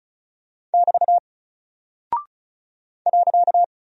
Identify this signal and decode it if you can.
Morse code, heard as a keyed tone, about 36 wpm, 700 Hz. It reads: <BT> .
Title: Single Character (Letter:Number:Punctuation:Prosign) Alt 35wpm